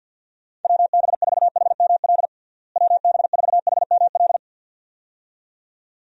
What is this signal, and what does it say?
Morse code, heard as a keyed tone, about 52 wpm, 700 Hz. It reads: WB4HML WB4HML
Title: USA Call Signs - 6 Characters - Repeated 2x - 50wpm